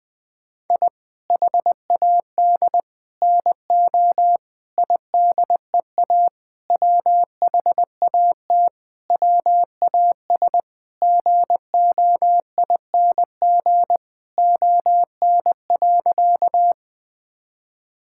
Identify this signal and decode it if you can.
Morse code, heard as a keyed tone, about 20 wpm, 700 Hz. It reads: I HAD NO IDEA WHAT WAS GOING ON.
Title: Sentences with 8-Word Phrases - Spoken Beginning - 20wpm